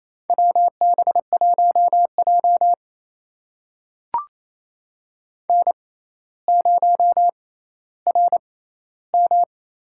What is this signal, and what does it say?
Morse code, heard as a keyed tone, about 28 wpm, spaced as for 14 wpm, 700 Hz. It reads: WB1J N0RM
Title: USA Call Signs - 4 Characters - 6x Character Spacing - 28wpm